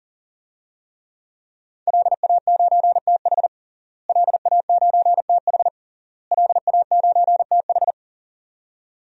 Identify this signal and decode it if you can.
Morse code, heard as a keyed tone, about 40 wpm, 700 Hz. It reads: LA9TH LA9TH LA9TH
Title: DXCC-100 Call Signs - 5 Characters - Repeated 3x - 40wpm